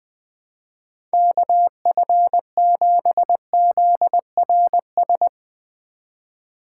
Morse code, heard as a keyed tone, about 20 wpm, 700 Hz. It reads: KF7ZRS
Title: USA Call Signs - 6 Characters - 4x Character Spacing - 20wpm